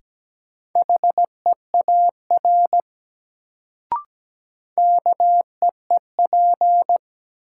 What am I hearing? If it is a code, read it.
Morse code, heard as a keyed tone, about 17 wpm, 700 Hz. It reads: HEAR KEEP
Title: Sentences from Top 300 Words - New Word Drill - 17wpm